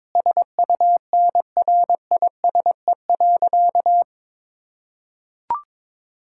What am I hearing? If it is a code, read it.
Morse code, heard as a keyed tone, about 22 wpm, 700 Hz. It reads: SUNRISE.
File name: Difficult Sentences with Hints - 22wpm